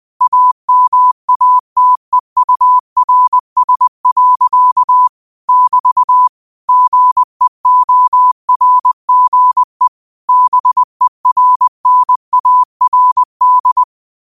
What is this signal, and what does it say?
Morse code, heard as a keyed tone, about 20 wpm, 1000 Hz. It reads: AMATEURS. <BT> GEORGE BERNARD
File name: qotdfiles / 20wpm